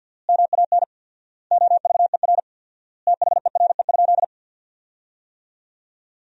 Morse code, heard as a keyed tone, about 47 wpm, 700 Hz. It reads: MAN OVER THERE?
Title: 4x Word-Spacing - Sentences from Top 200 Words - 50wpm